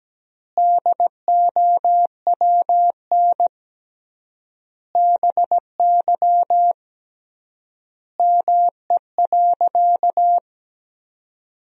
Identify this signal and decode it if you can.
Morse code, heard as a keyed tone, about 17 wpm, 700 Hz. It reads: DOWN BY ME.
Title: 3x Word-Spacing - Sentences from Top 500 Words - 17wpm